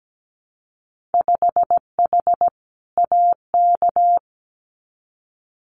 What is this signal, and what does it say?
Morse code, heard as a keyed tone, about 17 wpm, 700 Hz. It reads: 5H AK
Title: Winter Field Day Exchanges Encore 17wpm